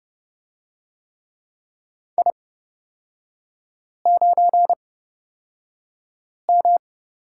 Morse code, heard as a keyed tone, about 30 wpm, spaced as for 5 wpm, 700 Hz. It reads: I9M
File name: Lesson 37e - Letter x - Three Characters 30wpm